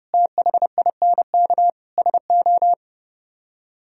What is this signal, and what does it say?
Morse code, heard as a keyed tone, about 37 wpm, 700 Hz. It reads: THINK SO
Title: Speed-Racing - Sets of 4 Words 30wpm